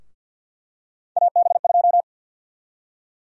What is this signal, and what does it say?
Morse code, heard as a keyed tone, about 52 wpm, 700 Hz. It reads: AB2
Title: USA Call Sign Prefixes - 50wpm